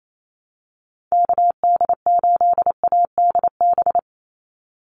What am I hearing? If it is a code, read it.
Morse code, heard as a keyed tone, about 28 wpm, 700 Hz. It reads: KD8ADB
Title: USA Call Signs 28wpm